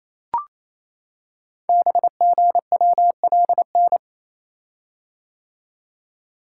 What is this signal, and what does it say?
Morse code, heard as a keyed tone, about 27 wpm, 700 Hz. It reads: BGWLN
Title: Code Groups - 5 Letters - 28wpm